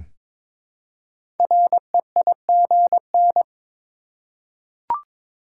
Morse code, heard as a keyed tone, about 22 wpm, 700 Hz. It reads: REIGN
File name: 5 Letter Words - First 3 Letters Given - 22wpm